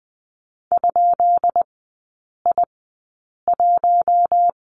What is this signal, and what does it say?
Morse code, heard as a keyed tone, about 20 wpm, spaced as for 8 wpm, 700 Hz. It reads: ?I1
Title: 133 Lesson 22e - Question Mark - Three Characters 20wpm